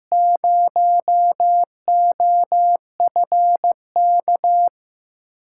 Morse code, heard as a keyed tone, about 15 wpm, 700 Hz. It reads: 0OFK
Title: USA Call Signs - 5 Characters - 6x Character Spacing - 15wpm